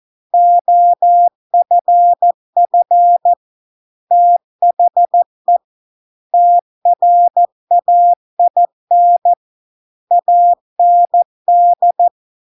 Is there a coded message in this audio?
Yes: Morse code from 0.3 to 12.1 s, heard as a keyed tone, about 14 wpm, 700 Hz. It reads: OFF THE TRAIN AND